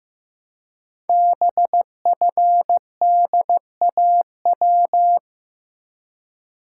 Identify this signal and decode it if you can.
Morse code, heard as a keyed tone, about 15 wpm, 690 Hz. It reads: BFDAW
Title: Code Groups - 5 Letters - 15wpm